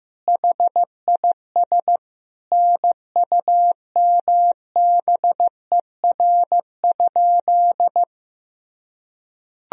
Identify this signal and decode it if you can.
Morse code, heard as a keyed tone, about 15 wpm, 700 Hz. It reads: HIS NUMBER?
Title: Sentences from Top 1000 Words - Review Part 1 - 15wpm